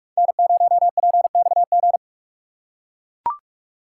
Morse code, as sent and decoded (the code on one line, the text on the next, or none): -. ----- .--. -..- --.
N0PXG